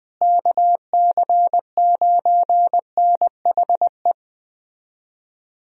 Morse code, heard as a keyed tone, about 20 wpm, 700 Hz. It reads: KC9NHE
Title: USA Call Signs - 6 Characters - Repeated 3x - 20wpm